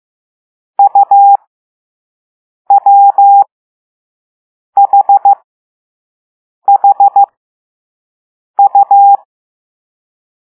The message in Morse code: UWHHU